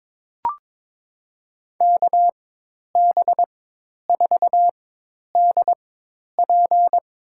Morse code, heard as a keyed tone, about 22 wpm, spaced as for 10 wpm, 700 Hz. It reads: KB4DP
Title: USA Call Signs - 4x Character Spacing - 22wpm